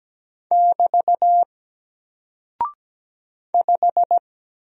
Morse code, heard as a keyed tone, about 17 wpm, 700 Hz. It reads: <BT> 5
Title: Single Character (Letter:Number:Punctuation:Prosign) Alt 17wpm